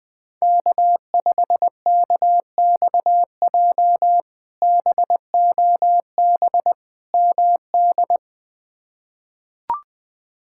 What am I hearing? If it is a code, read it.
Morse code, heard as a keyed tone, about 20 wpm, 700 Hz. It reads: K5KXJ BOB MD
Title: K1USN SST QSO Practice - 20wpm